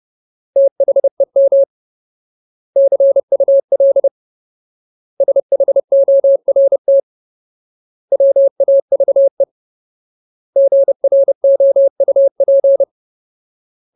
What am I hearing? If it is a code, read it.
Morse code, heard as a keyed tone, about 30 wpm, 550 Hz. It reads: THEM CUL SHORT WAVE GROUP
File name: Ultimate Word List / 30WPM Ultimate